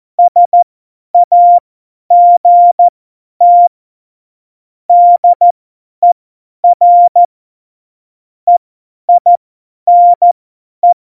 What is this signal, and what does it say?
Morse code, heard as a keyed tone, about 14 wpm, spaced as for 10 wpm, 700 Hz. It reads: SAGT DER EINE
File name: Morsekurs / Witze / Witz 14. August 2018, 50 F70